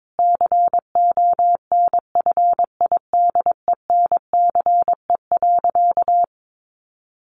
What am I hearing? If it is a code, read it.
Morse code, heard as a keyed tone, about 22 wpm, 700 Hz. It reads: CONFIDENCE.
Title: Sentences from Top 1500 Words 22wpm